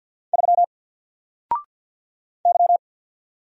Morse code, heard as a keyed tone, about 48 wpm, 700 Hz. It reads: 3 Y
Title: Single Character (Letter:Number:Punctuation:Prosign) 50wpm